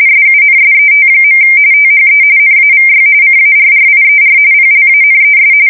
RTTY, about 45.45 baud. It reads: SUNSET. THE NAVY CARRIER PLANES
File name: fsk170 / USS-New-Jersey